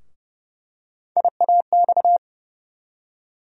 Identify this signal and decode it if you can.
Morse code, heard as a keyed tone, about 30 wpm, 700 Hz. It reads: IAX
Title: USA Call Sign Postfixes - 3 Letters 30wpm